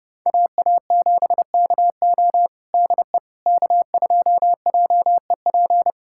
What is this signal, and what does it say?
Morse code, heard as a keyed tone, about 30 wpm, 700 Hz. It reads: AA7KO DE K2JEP
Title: USA Callsign DE Callsign - 5 Characters - 30wpm